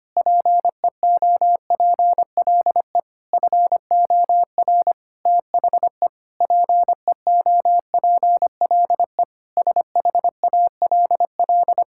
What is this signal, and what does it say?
Morse code, heard as a keyed tone, about 25 wpm, 700 Hz. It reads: PEOPLE FOR THE PEOPLE SHALL
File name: practice / 25wpm / Sentences with 9-Word Phrases - 25wpm